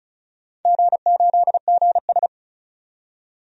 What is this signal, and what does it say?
Morse code, heard as a keyed tone, about 36 wpm, 700 Hz. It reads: G8GS
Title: DXCC-100 Call Signs - 4 Characters - 35wpm